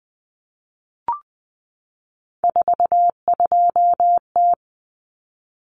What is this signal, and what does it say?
Morse code, heard as a keyed tone, about 20 wpm, 700 Hz. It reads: 42T
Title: Common Cut Numbers - 3-Digit 20wpm